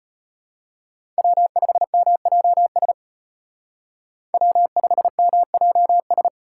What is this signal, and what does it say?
Morse code, heard as a keyed tone, about 30 wpm, 700 Hz. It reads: W5MJS W5MJS
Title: Speed Racing - USA 1x3 Call Signs 25wpm